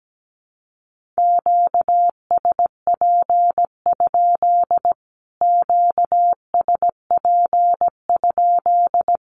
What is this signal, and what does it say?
Morse code, heard as a keyed tone, about 17 wpm, 700 Hz. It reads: QSP? QSP?
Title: Q-Codes Repeated 3x 17wpm